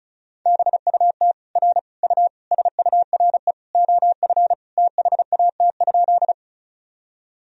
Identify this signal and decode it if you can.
Morse code, heard as a keyed tone, about 36 wpm, 700 Hz. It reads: BUT R U SURE OF THAT?
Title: Sentences from Top 300 Words and Abbreviations 35wpm